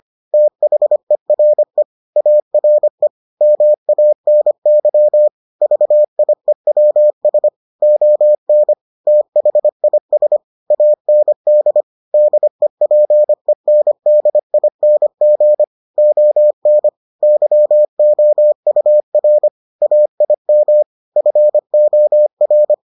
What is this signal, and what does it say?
Morse code, heard as a keyed tone, about 25 wpm, 600 Hz. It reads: THERE ARE MANY VIEWS ON THIS AND DEPENDING ON YOUR AIM FOR